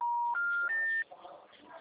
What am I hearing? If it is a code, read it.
Special information tone, with noise.
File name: out- -2006- - - .28916